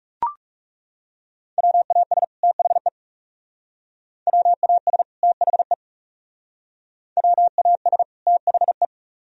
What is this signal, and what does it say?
Morse code, heard as a keyed tone, about 37 wpm, 700 Hz. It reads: WAS THE WAS THE WAS THE
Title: Speed-Racing - Sets of 2 Words 30wpm